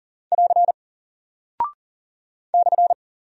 Morse code, heard as a keyed tone, about 40 wpm, 700 Hz. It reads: <AR> /
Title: Single Character (Letter:Number:Punctuation:Prosign) 40wpm